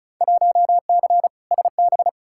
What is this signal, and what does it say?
Morse code, heard as a keyed tone, about 36 wpm, 690 Hz. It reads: 1C SB
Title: Field Day Exchanges Encore - 35wpm